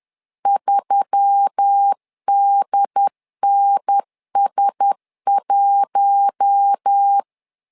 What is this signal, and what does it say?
Morse code, heard as a keyed tone, about 10 wpm, 800 Hz. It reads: 3DNS1